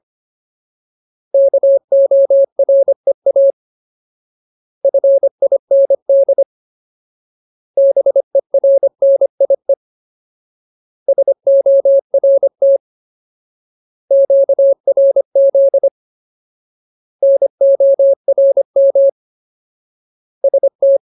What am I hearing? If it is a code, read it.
Morse code, heard as a keyed tone, about 25 wpm, 550 Hz. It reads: KOREA FIND BERNIE SORT QRZ NORM ST
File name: Ultimate Word List / 25WPM Ultimate